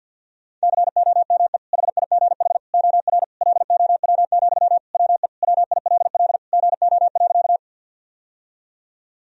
Morse code, heard as a keyed tone, about 52 wpm, 700 Hz. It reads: COME HIGH OR LOW, WE WILL GO.
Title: Sentences from Top 1000 Words - Review Part 1 - 50wpm